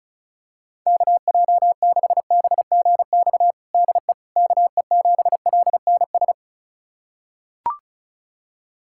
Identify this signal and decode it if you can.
Morse code, heard as a keyed tone, about 34 wpm, 700 Hz. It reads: KJ6BGX DE KE7LNS
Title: USA Callsign DE Callsign - 6 Characters - 35wpm